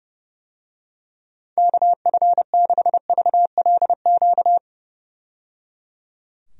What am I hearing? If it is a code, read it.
Morse code, heard as a keyed tone, about 30 wpm, 700 Hz. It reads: KF6VLQ
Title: USA Call Signs - 6 Characters - 30wpm